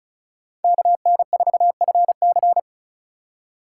Morse code, heard as a keyed tone, about 36 wpm, 700 Hz. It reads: KN4FC
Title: USA Call Signs - 4x Character Spacing - 35wpm